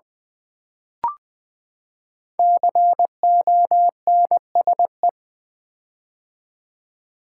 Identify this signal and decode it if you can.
Morse code, heard as a keyed tone, about 20 wpm, 700 Hz. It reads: CONSE
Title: Build A Word - 5 Letters At A Time - 12-Letter Words - 20wpm